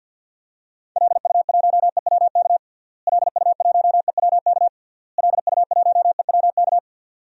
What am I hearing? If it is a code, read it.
Morse code, heard as a keyed tone, about 51 wpm, 700 Hz. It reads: LU1EWK LU1EWK LU1EWK
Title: DXCC-100 Call Signs - 6 Characters - Repeated 3x - 50wpm